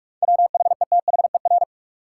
Morse code, heard as a keyed tone, about 46 wpm, 700 Hz. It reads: WHETHER
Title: Sentences from Top 1300 Words - New Word Drill Alt - 45wpm